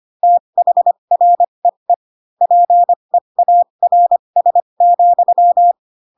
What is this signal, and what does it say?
Morse code, heard as a keyed tone, about 25 wpm, spaced as for 22 wpm, 700 Hz. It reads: THREE PEARS,
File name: Pinocchio / Pinocchio ch7 18 25 ik1hge